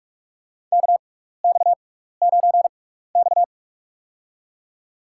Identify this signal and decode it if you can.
Morse code, heard as a keyed tone, about 47 wpm, spaced as for 16 wpm, 700 Hz. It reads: KX9X
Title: USA Call Signs - 6x Character Spacing - 45wpm